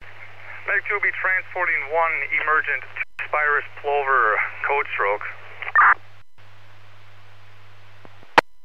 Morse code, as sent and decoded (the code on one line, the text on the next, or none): none